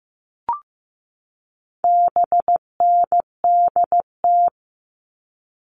Morse code, heard as a keyed tone, about 15 wpm, 700 Hz. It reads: BNDT
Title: All Cut Numbers - 4-Digit 15wpm